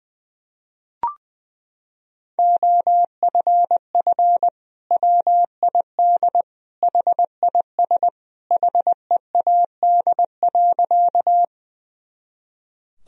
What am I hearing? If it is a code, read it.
Morse code, heard as a keyed tone, about 20 wpm, 700 Hz. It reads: OFF WID HIS HEAD.
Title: Sentences from Top 300 Words and Abbreviations 20wpm